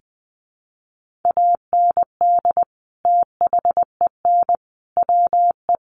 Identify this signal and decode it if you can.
Morse code, heard as a keyed tone, about 20 wpm, 700 Hz. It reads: AND THEN WE
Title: Sentences from Top 300 Words 20wpm